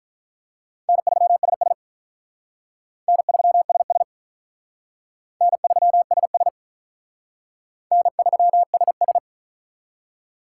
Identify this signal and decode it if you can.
Morse code, heard as a keyed tone, about 44 wpm, 700 Hz. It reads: N3SS N3SS N3SS N3SS